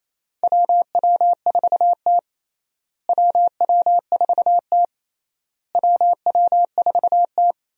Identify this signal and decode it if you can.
Morse code, heard as a keyed tone, about 28 wpm, 700 Hz. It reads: WW4T WW4T WW4T
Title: USA Call Signs - 4 Characters - Repeated 3x - 28wpm